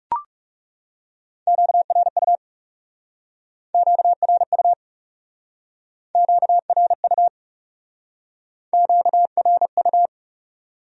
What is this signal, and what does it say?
Morse code, heard as a keyed tone, about 39 wpm, 700 Hz. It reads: QRU QRU QRU QRU